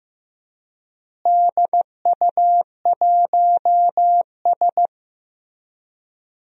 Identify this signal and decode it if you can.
Morse code, heard as a keyed tone, about 15 wpm, 700 Hz. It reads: DU1S
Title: DXCC-100 Call Signs - 4 Characters - Repeated 2x - 15wpm